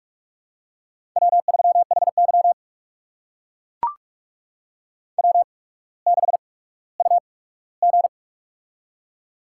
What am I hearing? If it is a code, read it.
Morse code, heard as a keyed tone, about 44 wpm, spaced as for 19 wpm, 690 Hz. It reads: W3HY W6UG
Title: USA Call Signs - 4 Characters - 8x Character Spacing - 45wpm